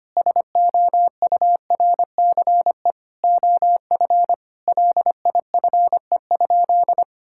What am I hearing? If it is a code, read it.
Morse code, heard as a keyed tone, about 25 wpm, 700 Hz. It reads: SOURCE OF LIFE?
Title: Sentences from Top 1000 Words - Review Part 5 - 25wpm